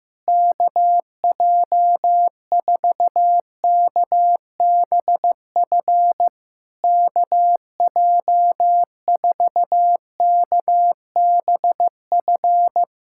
Morse code, heard as a keyed tone, about 15 wpm, 700 Hz. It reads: KJ4KBF KJ4KBF
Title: SOTA QSO Practice 15wpm